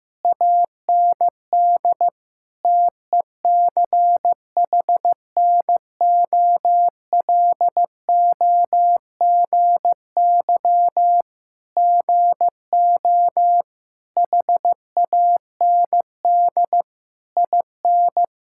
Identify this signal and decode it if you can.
Morse code, heard as a keyed tone, about 15 wpm, 700 Hz. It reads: AND TECHNOLOGY GO HAND IN
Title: Sentences from Top 1000 Words - Review Part 5 - 15wpm